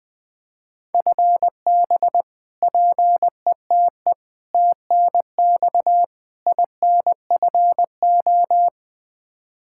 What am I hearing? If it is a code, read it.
Morse code, heard as a keyed tone, about 20 wpm, 700 Hz. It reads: FB PETE TNX INFO
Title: Practice QSO - Exchange 3 - More Info - 20wpm